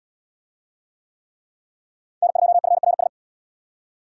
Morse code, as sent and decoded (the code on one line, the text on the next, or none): -. ..--- ..-. .-. ...
N2FRS